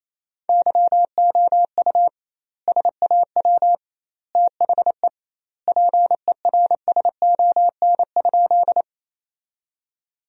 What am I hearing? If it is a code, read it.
Morse code, heard as a keyed tone, about 28 wpm, 700 Hz. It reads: YOU SAW THE PERSON?
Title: 2x Word-Spacing - Sentences from Top 400 Words - 28wpm